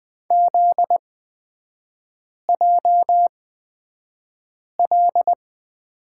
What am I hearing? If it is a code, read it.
Morse code, heard as a keyed tone, about 20 wpm, spaced as for 5 wpm, 700 Hz. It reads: ZJL
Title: 250 Lesson 40e - BK - Three Characters 20wpm